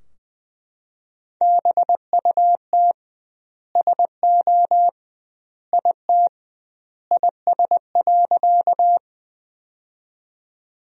Morse code, heard as a keyed tone, about 20 wpm, 700 Hz. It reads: BUT SO IT IS.
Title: 2x Word-Spacing - Sentences from Top 100 Words - 20wpm